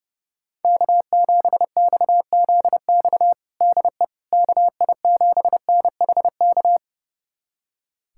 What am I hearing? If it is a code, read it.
Morse code, heard as a keyed tone, about 29 wpm, 700 Hz. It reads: K7XZX DE KI7NHK
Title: USA Callsign DE Callsign 30wpm